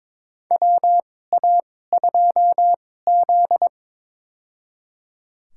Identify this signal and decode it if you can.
Morse code, heard as a keyed tone, about 22 wpm, spaced as for 16 wpm, 700 Hz. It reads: WA2Z
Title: USA Call Signs - 4 Characters - 2x Character Spacing - 22wpm